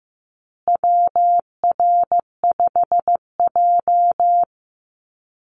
Morse code, heard as a keyed tone, about 15 wpm, 700 Hz. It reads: WR5J
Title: Call Signs Repeated 2x - 15wpm